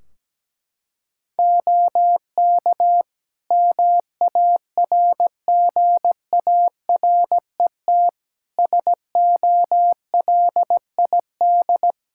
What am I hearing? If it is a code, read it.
Morse code, heard as a keyed tone, about 17 wpm, 700 Hz. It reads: OK MARGARET SOLID
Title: Practice QSO - Exchange 3 - More Info - 17wpm